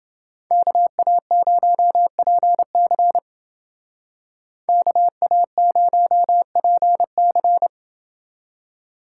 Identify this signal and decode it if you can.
Morse code, heard as a keyed tone, about 29 wpm, 700 Hz. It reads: KA0PC KA0PC